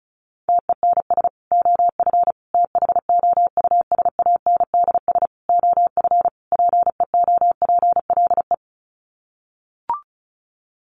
Morse code, heard as a keyed tone, about 34 wpm, 700 Hz. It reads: TENS OF THOUSANDS OF PEOPLE
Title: Sets of 5 Words Encore2 35wpm